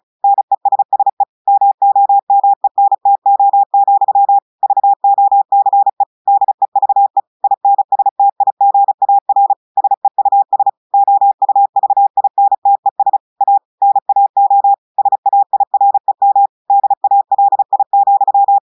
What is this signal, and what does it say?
Morse code, heard as a keyed tone, about 34 wpm, 800 Hz. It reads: NESSE MOMENTO, VOCE DEVE INSTIGAR SEUS OUVINTES A NAO SAIREM DALI,
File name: audios / portugais / portugais texte en clair 2 vitesse 35 mots minutes